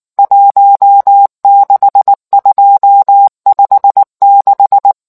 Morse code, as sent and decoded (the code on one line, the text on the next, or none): .---- -.... ..--- ..... -....
16256